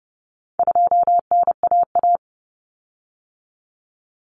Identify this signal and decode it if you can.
Morse code, heard as a keyed tone, about 30 wpm, 700 Hz. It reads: 2NAA
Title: Common Cut Numbers - 4-Digit 30wpm